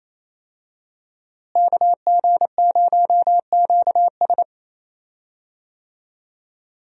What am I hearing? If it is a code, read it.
Morse code, heard as a keyed tone, about 28 wpm, 700 Hz. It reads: KG0QS
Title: USA Call Signs - 5 Characters - Repeated 2x - 28wpm